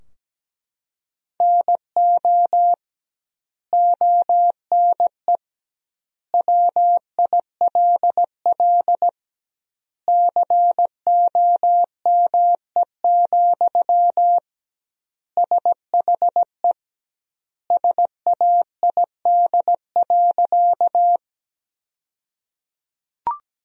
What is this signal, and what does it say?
Morse code, heard as a keyed tone, about 17 wpm, 700 Hz. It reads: NO ONE WILL COME, SHE SAID.
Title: 2x Word-Spacing - Sentences from Top 100 Words - 17wpm